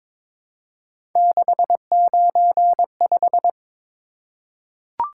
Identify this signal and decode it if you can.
Morse code, heard as a keyed tone, about 22 wpm, 700 Hz. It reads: 695
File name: practice / 22wpm / Three Letter-Number Combinations - 22wpm